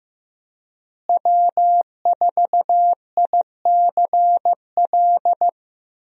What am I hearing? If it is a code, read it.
Morse code, heard as a keyed tone, about 16 wpm, 700 Hz. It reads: W4ICL
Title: USA Call Signs - 5 Characters - Repeated 3x - 15wpm